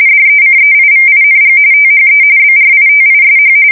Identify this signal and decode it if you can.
RTTY, about 45.45 baud. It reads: DECEIVING THE BRITISH
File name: fsk170 / usnavy1